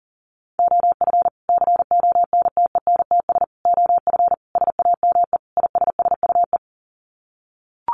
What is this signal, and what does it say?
Morse code, heard as a keyed tone, about 40 wpm, 700 Hz. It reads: OF CONTENTS OF SAME ISSUE
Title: Sets of 5 Words Encore2 40wpm